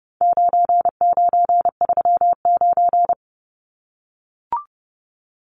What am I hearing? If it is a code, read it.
Morse code, heard as a keyed tone, about 30 wpm, 700 Hz. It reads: 9939
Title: Numbers - 4-Digit 30wpm